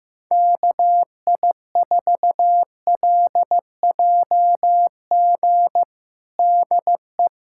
Morse code, heard as a keyed tone, about 15 wpm, 700 Hz. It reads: KI4LJG DE
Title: USA Callsign DE Callsign 15wpm